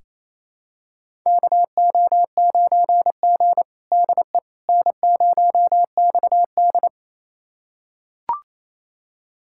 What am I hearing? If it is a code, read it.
Morse code, heard as a keyed tone, about 28 wpm, 700 Hz. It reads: KO9G DE N0XD
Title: USA Callsign DE Callsign - 4 Characters - 28wpm